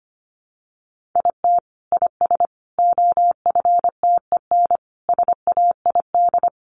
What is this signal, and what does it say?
Morse code, heard as a keyed tone, about 25 wpm, 700 Hz. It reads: IT IS OFTEN SAID
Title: Sentences from Top 1400 Words 25wpm